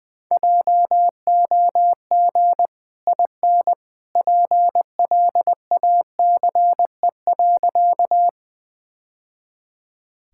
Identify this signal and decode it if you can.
Morse code, heard as a keyed tone, about 20 wpm, 700 Hz. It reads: JOG IN PLACE.
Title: Easy Sentences with Hints - 20wpm